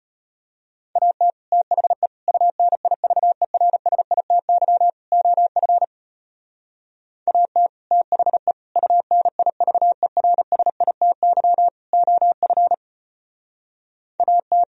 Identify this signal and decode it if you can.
Morse code, heard as a keyed tone, about 32 wpm, 700 Hz. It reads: AT THE UNIVERSITY OF AT THE UNIVERSITY OF AT